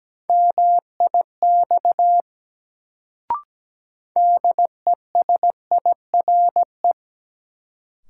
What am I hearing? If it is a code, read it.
Morse code, heard as a keyed tone, about 17 wpm, 700 Hz. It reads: MIX DESIRE
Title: Sentences from Top 1100 Words - New Word Drill Alt - 17wpm